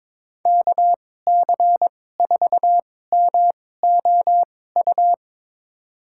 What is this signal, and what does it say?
Morse code, heard as a keyed tone, about 22 wpm, spaced as for 16 wpm, 700 Hz. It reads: KC4MOU
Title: USA Call Signs - 2x Character Spacing - 22wpm